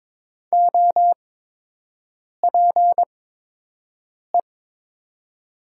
Morse code, heard as a keyed tone, about 22 wpm, spaced as for 6 wpm, 700 Hz. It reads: OPE
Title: 8x Character Spacing - Top 300 Words - 22wpm